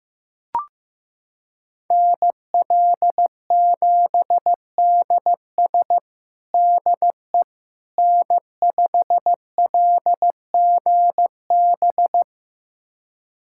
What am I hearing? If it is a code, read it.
Morse code, heard as a keyed tone, about 15 wpm, 700 Hz. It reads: NL7DS DE N5LGB
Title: USA Callsign DE Callsign - 5 Characters - 15wpm